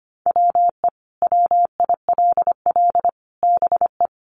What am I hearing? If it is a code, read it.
Morse code, heard as a keyed tone, about 25 wpm, 700 Hz. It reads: WE WILL BE
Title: Sets of 5 Words Encore 25wpm